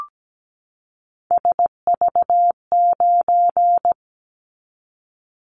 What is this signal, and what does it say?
Morse code, heard as a keyed tone, about 17 wpm, 700 Hz. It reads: SV9
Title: Callsign Prefixes for Top 100 DXCC - 17wpm